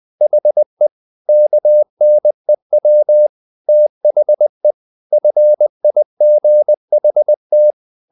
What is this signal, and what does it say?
Morse code, heard as a keyed tone, about 20 wpm, 600 Hz. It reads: HE KNEW THE FIGHT
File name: the old man and the sea part 2 20wpm